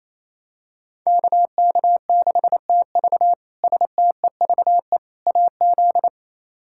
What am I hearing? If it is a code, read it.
Morse code, heard as a keyed tone, about 28 wpm, 700 Hz. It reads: KK6TV STEVE AZ
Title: K1USN SST QSO Practice - 28wpm